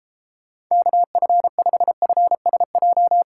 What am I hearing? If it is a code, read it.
Morse code, heard as a keyed tone, about 33 wpm, 700 Hz. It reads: KF5FSJ
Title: Speed Racing - USA 2x3 Call Signs 28wpm